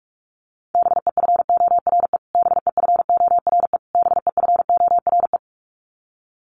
Morse code, heard as a keyed tone, about 44 wpm, 700 Hz. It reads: BEFORE BEFORE BEFORE
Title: Top 300 Words Repeated 45wpm